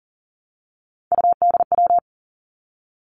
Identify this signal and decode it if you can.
Morse code, heard as a keyed tone, about 38 wpm, 690 Hz. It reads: UDW
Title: Three Challenging Letter-Number Combinations 40wpm